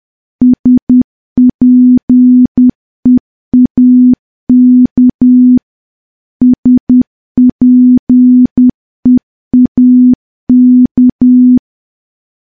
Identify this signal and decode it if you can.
Morse code, heard as a keyed tone, about 10 wpm, 250 Hz. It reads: SPEAK SPEAK
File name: morse / morse-training-10WPM